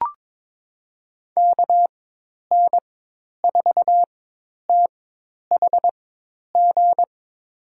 Morse code, heard as a keyed tone, about 23 wpm, spaced as for 10 wpm, 700 Hz. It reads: KN4THG
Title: USA Call Signs - 4x Character Spacing - 22wpm